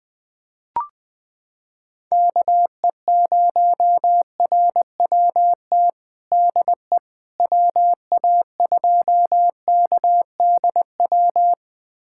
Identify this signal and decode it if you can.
Morse code, heard as a keyed tone, about 20 wpm, 700 Hz. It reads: KE0RWT DE WA2KDW
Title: USA Callsign DE Callsign - 6 Characters - 20wpm